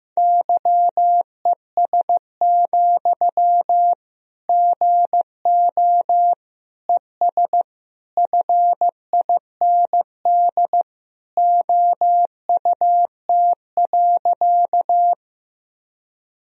Morse code, heard as a keyed tone, about 15 wpm, 700 Hz. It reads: YES, GO ES FIND OUT.
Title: Sentences from Top 500 Words and Abbreviations 15wpm